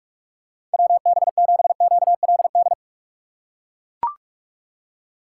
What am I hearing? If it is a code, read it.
Morse code, heard as a keyed tone, about 41 wpm, 700 Hz. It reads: WB7QLD
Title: USA Call Signs - 4x Character Spacing - 45wpm